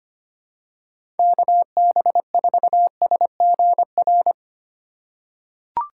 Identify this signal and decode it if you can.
Morse code, heard as a keyed tone, about 25 wpm, 700 Hz. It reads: KB4SGR
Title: USA Call Signs - 6 Characters - 4x Character Spacing - 25wpm